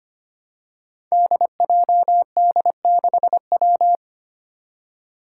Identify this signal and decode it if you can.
Morse code, heard as a keyed tone, about 25 wpm, 700 Hz. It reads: DJD6W
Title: Five Challenging Letter-Number Combinations 25wpm